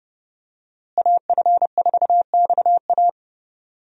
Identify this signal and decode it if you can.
Morse code, heard as a keyed tone, about 30 wpm, 700 Hz. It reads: AF4XA
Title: USA Call Signs - 5 Characters - Repeated 2x - 30wpm